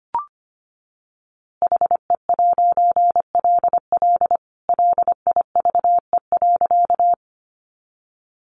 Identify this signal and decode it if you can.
Morse code, heard as a keyed tone, about 25 wpm, 700 Hz. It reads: HE'LL LIVE.
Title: A Sound Of Thunder 25wpm